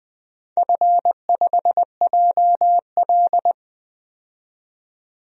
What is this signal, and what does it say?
Morse code, heard as a keyed tone, about 20 wpm, 700 Hz. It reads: F5JL
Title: DXCC-100 Call Signs - 4 Characters - 20wpm